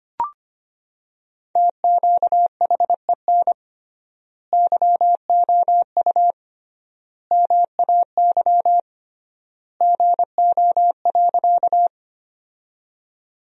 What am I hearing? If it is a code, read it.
Morse code, heard as a keyed tone, about 25 wpm, 700 Hz. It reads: TQHEN YOU MAY GO.
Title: 3x Word-Spacing - Sentences from Top 100 Words - 25wpm